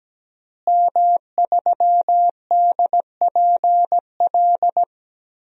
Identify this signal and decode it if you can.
Morse code, heard as a keyed tone, about 17 wpm, 700 Hz. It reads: M3DPL
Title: DXCC-100 Call Signs - 5 Characters - Repeated 2x - 17wpm